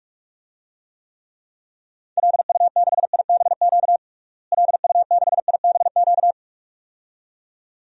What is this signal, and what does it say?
Morse code, heard as a keyed tone, about 46 wpm, 700 Hz. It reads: LU6IBQ LU6IBQ
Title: DXCC-100 Call Signs - 6 Characters - Repeated 2x - 45wpm